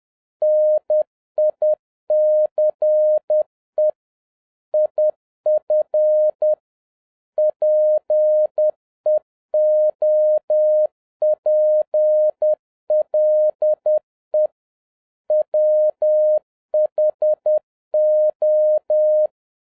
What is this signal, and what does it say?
Morse code, heard as a keyed tone, about 10 wpm, 610 Hz. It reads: NICE IF PEOPLE WHO